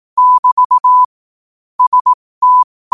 Morse code, heard as a keyed tone, about 18 wpm, spaced as for 15 wpm, 1000 Hz. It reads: <BT> ST
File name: qotdfiles / 15wpm